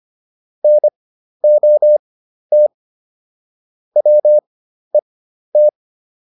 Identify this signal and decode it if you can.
Morse code, heard as a keyed tone, about 25 wpm, spaced as for 12 wpm, 600 Hz. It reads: NOT WET